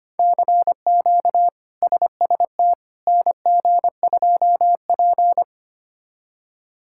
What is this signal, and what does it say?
Morse code, heard as a keyed tone, about 25 wpm, 700 Hz. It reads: CQ SST NG2P
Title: K1USN SST QSO Practice - 25wpm